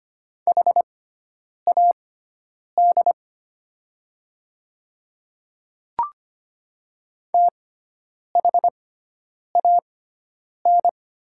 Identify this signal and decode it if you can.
Morse code, heard as a keyed tone, about 25 wpm, spaced as for 9 wpm, 700 Hz. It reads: HAD THAN
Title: 6x Character Spacing - Top 300 Words - 25wpm